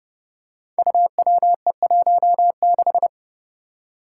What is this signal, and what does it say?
Morse code, heard as a keyed tone, about 30 wpm, 700 Hz. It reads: UWE16
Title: Five Challenging Letter-Number Combinations 30wpm